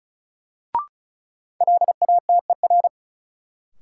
Morse code, heard as a keyed tone, about 36 wpm, 700 Hz. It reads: LATER
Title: Sentences from Top 700 Words - New Word Drill - 35wpm